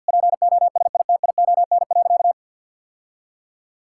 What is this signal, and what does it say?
Morse code, heard as a keyed tone, about 50 wpm, 700 Hz. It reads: POSITION.
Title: Sentences from Top 1000 Words - Review Part 5 - 50wpm